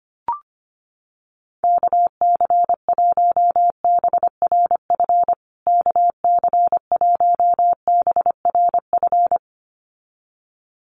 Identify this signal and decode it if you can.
Morse code, heard as a keyed tone, about 25 wpm, 700 Hz. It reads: KC1BRF KC1BRF
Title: USA Call Signs Repeated 2x 25wpm